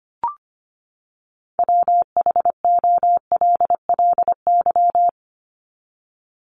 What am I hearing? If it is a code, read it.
Morse code, heard as a keyed tone, about 25 wpm, 700 Hz. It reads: WHOLLY
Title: LY Suffix - Top 100 Words with Base Words 25wpm